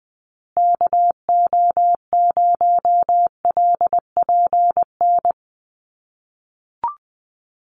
Morse code, heard as a keyed tone, about 19 wpm, 700 Hz. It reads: KO0LPN
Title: USA Call Signs Repeated 3x 20wpm